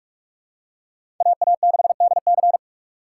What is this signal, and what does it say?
Morse code, heard as a keyed tone, about 46 wpm, 700 Hz. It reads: AA6DC
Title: USA Call Signs - 5 Characters - 3x Character Spacing - 45wpm